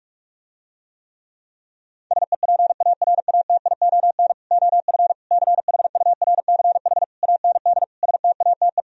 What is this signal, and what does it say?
Morse code, heard as a keyed tone, about 48 wpm, 700 Hz. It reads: SEPARATION OF CHURCH AND STATE